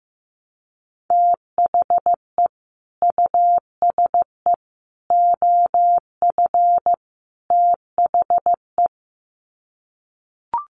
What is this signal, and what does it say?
Morse code, heard as a keyed tone, about 15 wpm, 700 Hz. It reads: THE USE OF THE
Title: Sets of 4 Words 15wpm